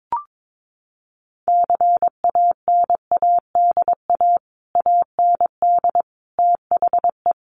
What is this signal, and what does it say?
Morse code, heard as a keyed tone, about 22 wpm, 700 Hz. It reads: CANADA AND THE
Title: Sets of 5 Words Encore 22wpm